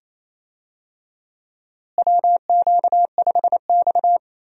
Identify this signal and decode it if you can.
Morse code, heard as a keyed tone, about 28 wpm, 700 Hz. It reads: WQ5X
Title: USA Call Signs - 4 Characters - Repeated 2x - 28wpm